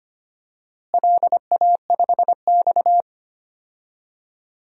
Morse code, heard as a keyed tone, about 25 wpm, 700 Hz. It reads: LA5X
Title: DXCC-100 Call Signs - 4 Characters - Repeated 3x - 25wpm